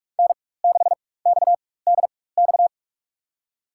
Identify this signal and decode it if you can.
Morse code, heard as a keyed tone, about 46 wpm, spaced as for 21 wpm, 700 Hz. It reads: N6XDX
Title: USA Call Signs - 5 Characters - 4x Character Spacing - 45wpm